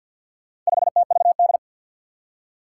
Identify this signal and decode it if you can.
Morse code, heard as a keyed tone, about 51 wpm, 700 Hz. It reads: 5TVD
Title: Four Challenging Letter-Number Combinations 50wpm